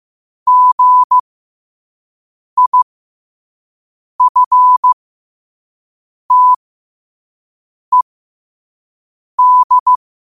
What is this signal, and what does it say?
Morse code, heard as a keyed tone, about 15 wpm, spaced as for 5 wpm, 1000 Hz. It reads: GIFTED
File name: qotdfiles / 5wpm